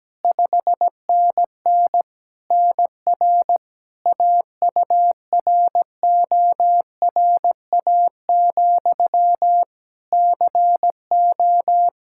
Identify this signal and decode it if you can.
Morse code, heard as a keyed tone, about 17 wpm, 700 Hz. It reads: 5NN NR AURORA, CO
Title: Practice QSO - Exchange 1 - Beginning - 17wpm